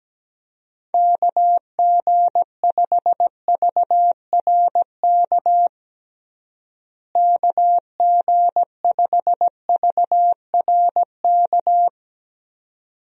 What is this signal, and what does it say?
Morse code, heard as a keyed tone, about 17 wpm, 700 Hz. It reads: KG5VRK KG5VRK
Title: USA Call Signs - 6 Characters - Repeated 2x - 17wpm